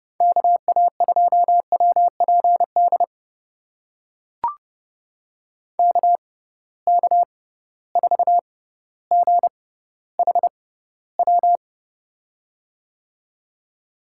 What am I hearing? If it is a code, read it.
Morse code, heard as a keyed tone, about 30 wpm, spaced as for 15 wpm, 700 Hz. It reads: KA2WPD KK4GHW
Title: USA Call Signs - 6x Character Spacing - 30wpm